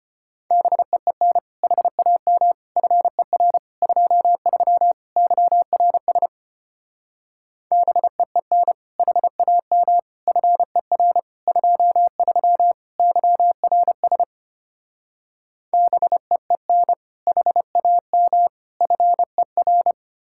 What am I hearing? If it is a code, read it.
Morse code, heard as a keyed tone, about 33 wpm, 700 Hz. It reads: BEEN HAM FER 23 YRS BEEN HAM FER 23 YRS BEEN HAM FER